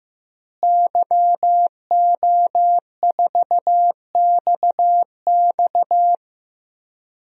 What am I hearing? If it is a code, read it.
Morse code, heard as a keyed tone, about 15 wpm, 700 Hz. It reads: YO4XX
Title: DXCC-100 Call Signs - 5 Characters - Repeated 3x - 15wpm